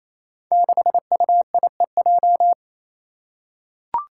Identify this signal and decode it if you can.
Morse code, heard as a keyed tone, about 28 wpm, 700 Hz. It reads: 6UIEJ
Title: Five Challenging Letter-Number Combinations 28wpm